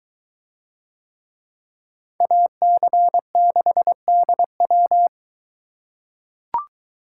Morse code, heard as a keyed tone, about 22 wpm, 700 Hz. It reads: AC6DW